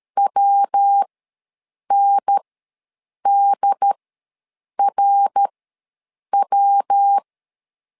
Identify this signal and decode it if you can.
Morse code, heard as a keyed tone, about 13 wpm, spaced as for 7 wpm, 800 Hz. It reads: WNDRW
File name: LEZIONI / 3 MWRO / 3 MWR ws 12wpm 1809